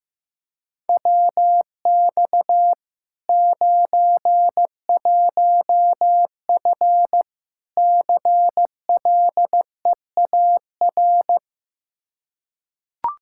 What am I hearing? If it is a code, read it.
Morse code, heard as a keyed tone, about 16 wpm, 700 Hz. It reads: WX 91F CLEAR
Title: K1USN SST QSO Practice 15wpm